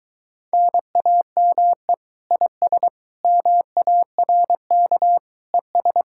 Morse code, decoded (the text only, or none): NAME IS MARK ES